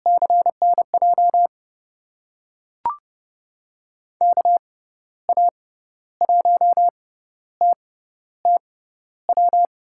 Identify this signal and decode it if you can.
Morse code, heard as a keyed tone, about 30 wpm, spaced as for 13 wpm, 700 Hz. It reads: CNJ KA1TTW
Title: USA Call Signs - 6 Characters - 6x Character Spacing - 30wpm